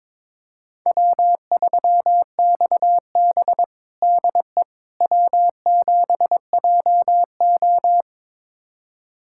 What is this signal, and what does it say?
Morse code, heard as a keyed tone, about 23 wpm, 700 Hz. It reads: W3XB DE W7JO
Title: USA Callsign DE Callsign - 4 Characters - 22wpm